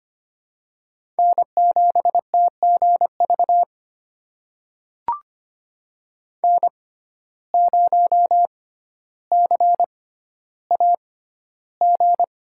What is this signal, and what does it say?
Morse code, heard as a keyed tone, about 25 wpm, spaced as for 13 wpm, 700 Hz. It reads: N7TGV N0CAG
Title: USA Call Signs - 5 Characters - 6x Character Spacing - 25wpm